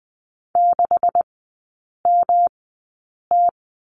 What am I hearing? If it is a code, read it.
Morse code, heard as a keyed tone, about 20 wpm, spaced as for 8 wpm, 700 Hz. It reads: 6MT
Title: 133 Lesson 22e - Question Mark - Three Characters 20wpm